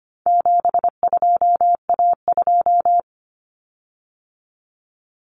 Morse code, heard as a keyed tone, about 25 wpm, 700 Hz. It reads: 72A2
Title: Common Cut Numbers - 4-Digit 25wpm